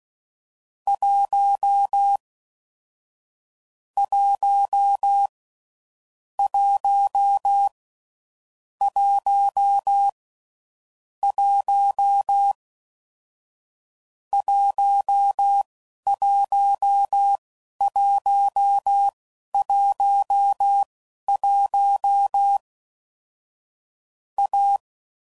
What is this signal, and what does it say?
Morse code, heard as a keyed tone, about 16 wpm, spaced as for 11 wpm, 800 Hz. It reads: 1 1 1 1 1 11111 A
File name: Combinations / Difficult Combos / AWJ1